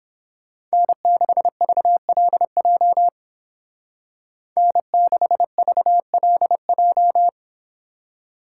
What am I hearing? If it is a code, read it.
Morse code, heard as a keyed tone, about 28 wpm, 700 Hz. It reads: N6VLJ N6VLJ